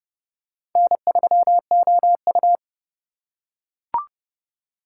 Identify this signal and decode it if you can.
Morse code, heard as a keyed tone, about 30 wpm, 700 Hz. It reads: N3OU
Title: USA Call Signs - 3x Character Spacing - 30wpm